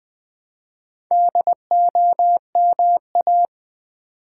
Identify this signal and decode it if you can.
Morse code, heard as a keyed tone, about 20 wpm, 700 Hz. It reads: DOMA
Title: Build A Word - 2 Letters At A Time - 6-Letter Words - 20wpm